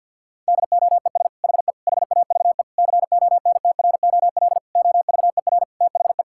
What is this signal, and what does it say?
Morse code, heard as a keyed tone, about 52 wpm, 700 Hz. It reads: DOES HE HAVE CONTROL OVER THE
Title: Sentences from Top 1000 Words - Review Part 5 - 50wpm